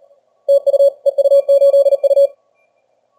Morse code, heard as a keyed tone, about 35 wpm, 560 Hz. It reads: TU EU8U
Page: Сегодня решил проверить динамический диапазон приемника в экстримальных условиях теста. Плотность станций на диапазонах большая.